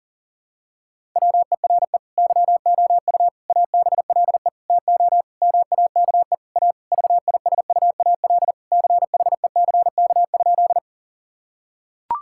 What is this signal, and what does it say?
Morse code, heard as a keyed tone, about 40 wpm, 700 Hz. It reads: WERE YOU ABLE TO MAKE A VISUAL CHECK?
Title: Sentences from Top 1000 Words - Review Part 6 - 40wpm